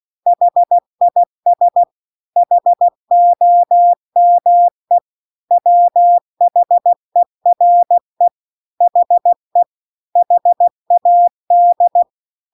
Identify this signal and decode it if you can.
Morse code, heard as a keyed tone, about 16 wpm, 700 Hz. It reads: HIS HOME WHERE HE HAD